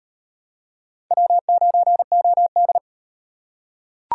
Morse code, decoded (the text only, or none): W9OD